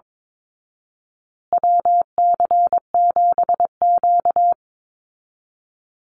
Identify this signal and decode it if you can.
Morse code, heard as a keyed tone, about 22 wpm, 700 Hz. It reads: WC7Q
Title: Call Signs Repeated 2x - 22wpm